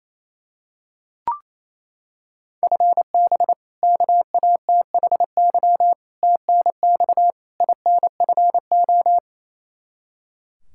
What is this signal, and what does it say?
Morse code, heard as a keyed tone, about 28 wpm, 700 Hz. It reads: FB KATHY TNX INFO
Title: Practice QSO - Exchange 3 - More Info - 28wpm